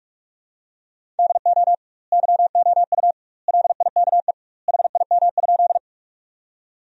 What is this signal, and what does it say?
Morse code, heard as a keyed tone, about 46 wpm, 700 Hz. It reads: DO YOU LIKE HIM?
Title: 2x Word-Spacing - Sentences from Top 100 Words - 45wpm